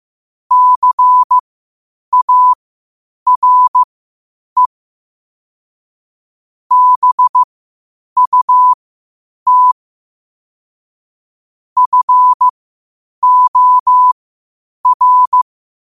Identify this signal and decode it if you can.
Morse code, heard as a keyed tone, about 15 wpm, spaced as for 8 wpm, 1000 Hz. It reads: CARE BUT FOR
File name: qotdfiles / 8wpm